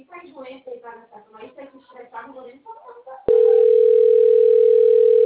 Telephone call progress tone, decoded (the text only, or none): ringback tone